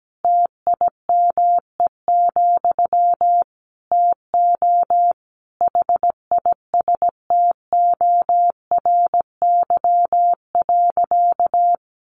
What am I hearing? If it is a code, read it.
Morse code, heard as a keyed tone, about 17 wpm, 700 Hz. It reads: TIME, TO HISTORY.
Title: A Sound Of Thunder 17wpm